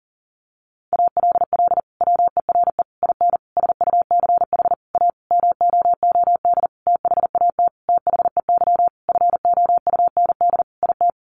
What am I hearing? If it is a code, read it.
Morse code, heard as a keyed tone, about 40 wpm, 700 Hz. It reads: ALL WERE IN SUCH A MOOD THAT THEY FOUND IT
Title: Sentences from Top 1100 Words 40wpm